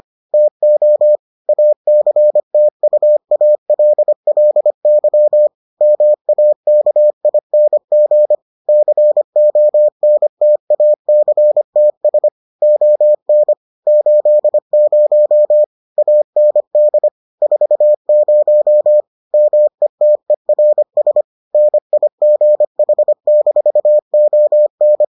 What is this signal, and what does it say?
Morse code, heard as a keyed tone, about 25 wpm, 600 Hz. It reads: TO ACTUALLY MAKING CONTACTS ON 80 AND 40 METERS NIGH-ON